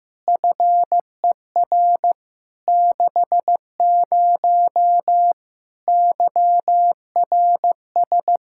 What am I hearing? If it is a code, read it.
Morse code, heard as a keyed tone, about 15 wpm, 700 Hz. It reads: FER 60 YRS
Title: Practice QSO - Exchange 3 - More Info - 15wpm